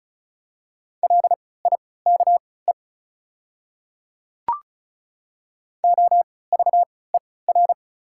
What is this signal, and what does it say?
Morse code, heard as a keyed tone, about 36 wpm, spaced as for 20 wpm, 700 Hz. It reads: LIKE OVER
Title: 3x Character Spacing - Top 100 Words - 35wpm